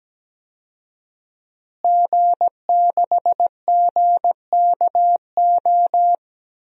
Morse code, heard as a keyed tone, about 17 wpm, 700 Hz. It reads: G6GKO
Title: DXCC-100 Call Signs - 5 Characters - Repeated 3x - 17wpm